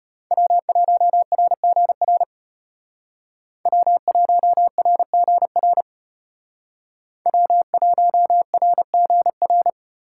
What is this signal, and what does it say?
Morse code, heard as a keyed tone, about 33 wpm, 700 Hz. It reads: W1RGR W1RGR W1RGR